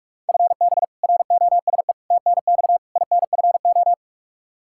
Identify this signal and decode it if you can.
Morse code, heard as a keyed tone, about 42 wpm, 700 Hz. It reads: FB ROSE TNX INFO
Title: Practice QSO - Exchange 3 - More Info - 45wpm